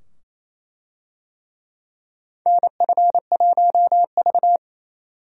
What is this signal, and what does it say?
Morse code, heard as a keyed tone, about 28 wpm, 700 Hz. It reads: NF1V
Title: USA Call Signs - 4 Characters - 28wpm